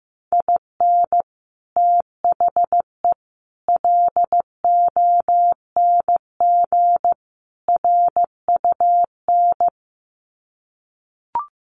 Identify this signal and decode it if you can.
Morse code, heard as a keyed tone, about 15 wpm, 700 Hz. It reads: IN THE LONG RUN
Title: Sets of 4 Words 15wpm